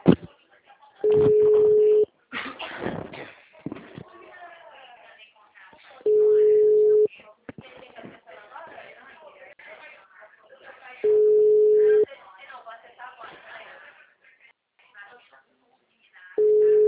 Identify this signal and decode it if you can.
Ringback tone (Europe, 425 Hz), with speech.
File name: out- -2007- - - .22245